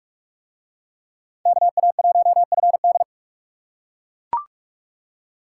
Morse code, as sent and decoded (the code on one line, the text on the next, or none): -.- .- .---- ..-. -..
KA1FD